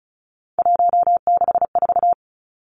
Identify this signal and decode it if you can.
Morse code, heard as a keyed tone, about 34 wpm, 700 Hz. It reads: 164
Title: Numbers - 3-Digit 35wpm